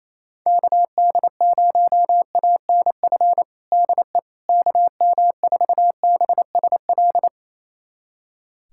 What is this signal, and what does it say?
Morse code, heard as a keyed tone, about 28 wpm, 700 Hz. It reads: KD0ANF DE KM4BSL
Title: USA Callsign DE Callsign - 6 Characters - 28wpm